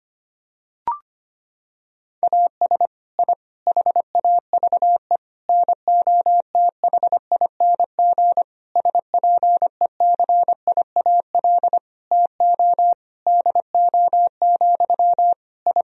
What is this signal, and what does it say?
Morse code, heard as a keyed tone, about 25 wpm, 700 Hz. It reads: AS I HAVE NOTHING SPECIAL TO DO, I
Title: Sentences from Top 1000 Words - Review Part 3 - 25wpm